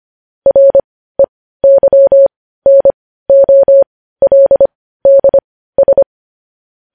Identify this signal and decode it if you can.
Morse code, heard as a keyed tone, about 25 wpm, spaced as for 15 wpm, 550 Hz. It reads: REYNOLDS
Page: news headlines in morse code at 15 wpm